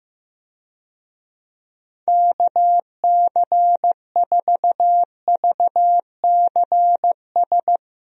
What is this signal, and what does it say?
Morse code, heard as a keyed tone, about 16 wpm, 700 Hz. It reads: KC4VCS
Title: USA Call Signs - 6 Characters - Repeated 2x - 15wpm